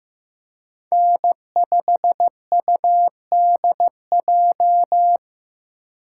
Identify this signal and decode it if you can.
Morse code, heard as a keyed tone, about 15 wpm, 700 Hz. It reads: N5UDJ
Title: USA Call Signs - 5 Characters - 4x Character Spacing - 15wpm